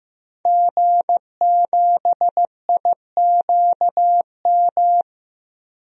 Morse code, heard as a keyed tone, about 15 wpm, 700 Hz. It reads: G7IQM
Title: DXCC-100 Call Signs - 5 Characters - 15wpm